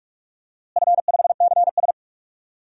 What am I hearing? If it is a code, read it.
Morse code, heard as a keyed tone, about 42 wpm, 700 Hz. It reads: F5CS
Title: DXCC-100 Call Signs - 4 Characters - Repeated 3x - 45wpm